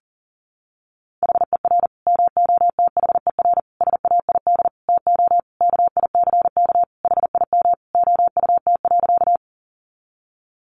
Morse code, heard as a keyed tone, about 37 wpm, 700 Hz. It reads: HER MOTHER SAID TO KICK HIM OUT.
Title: Sentences from Top 1300 Words 40wpm